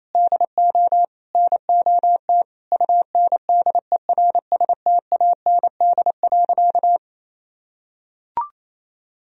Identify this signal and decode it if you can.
Morse code, heard as a keyed tone, about 28 wpm, 700 Hz. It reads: DO NOT UNDERSTAND.
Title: Sentences from Top 1000 Words - Review Part 1 - 28wpm